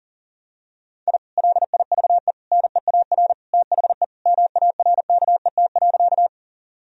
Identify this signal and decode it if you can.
Morse code, heard as a keyed tone, about 40 wpm, 700 Hz. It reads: I LIVE NEAR THE MARKET.
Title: Sentences from Top 1000 Words - Review Part 3 - 40wpm